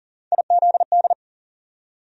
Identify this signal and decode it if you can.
Morse code, heard as a keyed tone, about 40 wpm, 700 Hz. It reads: IZD
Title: USA Call Sign Postfixes - 3 Letters 40wpm